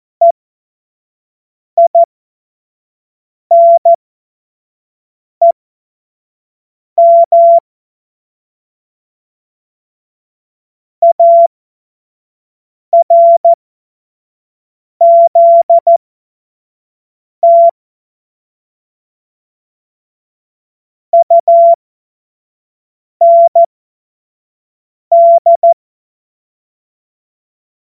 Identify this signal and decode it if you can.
Morse code, heard as a keyed tone, about 14 wpm, spaced as for 5 wpm, 680 Hz. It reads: EINEM ARZT UND
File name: Morsekurs / Witze / Witz 14.November 2023 25 F70